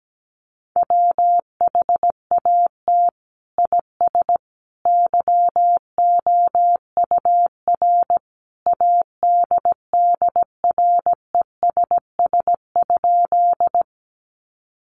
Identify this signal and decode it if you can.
Morse code, heard as a keyed tone, about 17 wpm, 700 Hz. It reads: WHAT IS YOUR ADDRESS?
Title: Sentences from Top 1000 Words 17wpm